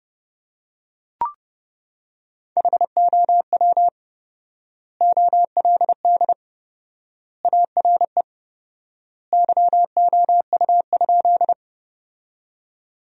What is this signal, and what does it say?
Morse code, heard as a keyed tone, about 30 wpm, 700 Hz. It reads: HOW OLD ARE YOU?
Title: 4x Word-Spacing - Sentences from Top 200 Words - 30wpm